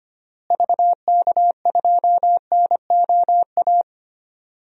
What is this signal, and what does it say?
Morse code, heard as a keyed tone, about 25 wpm, 700 Hz. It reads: VK2NOA
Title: DXCC-100 Call Signs - 6 Characters - Repeated 2x - 25wpm